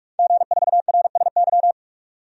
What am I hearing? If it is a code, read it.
Morse code, heard as a keyed tone, about 46 wpm, 700 Hz. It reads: G4RSY
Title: DXCC-100 Call Signs - 5 Characters - 45wpm